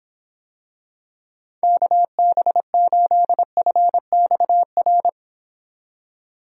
Morse code, heard as a keyed tone, about 24 wpm, 700 Hz. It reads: KB8FXR